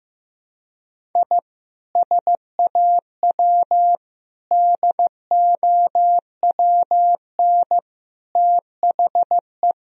Morse code, heard as a keyed tone, about 15 wpm, 700 Hz. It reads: I SAW DOWN THE
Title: Sentences from Top 1000 Words - Review Part 1 - 15wpm